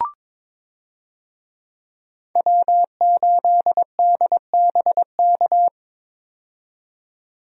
Morse code, heard as a keyed tone, about 22 wpm, 700 Hz. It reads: W8DBK
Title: USA Call Signs - 5 Characters - 22wpm